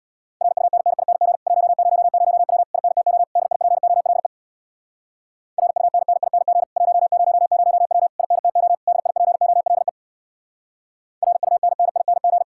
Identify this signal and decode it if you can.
Morse code, heard as a keyed tone, about 65 wpm, 700 Hz. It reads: RUNNING 100W INTO DIPOLE RUNNING 100W INTO DIPOLE RUNNING